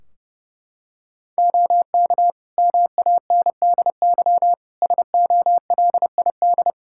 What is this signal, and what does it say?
Morse code, heard as a keyed tone, about 32 wpm, 700 Hz. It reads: OK MANDY SOLID
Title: Practice QSO - Exchange 3 - More Info - 30wpm